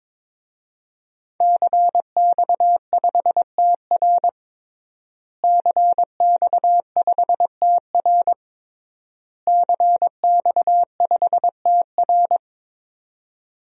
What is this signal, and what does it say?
Morse code, heard as a keyed tone, about 22 wpm, 700 Hz. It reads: CX5TR CX5TR CX5TR
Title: DXCC-100 Call Signs - 5 Characters - Repeated 3x - 22wpm